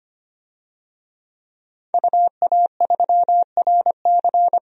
Morse code, heard as a keyed tone, about 25 wpm, 700 Hz. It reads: UA3RC
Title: DXCC-100 Call Signs - 5 Characters - 25wpm